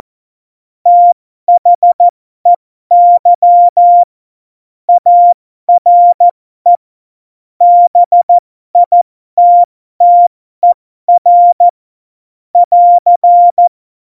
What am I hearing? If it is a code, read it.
Morse code, heard as a keyed tone, about 14 wpm, spaced as for 12 wpm, 700 Hz. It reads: THEY ARE BITTER <AR>
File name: Morsekurs / Witze